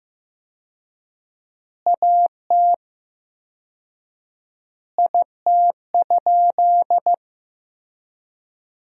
Morse code, heard as a keyed tone, about 15 wpm, 700 Hz. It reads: AT IT?
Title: 4x Word-Spacing - Sentences from Top 100 Words - 15wpm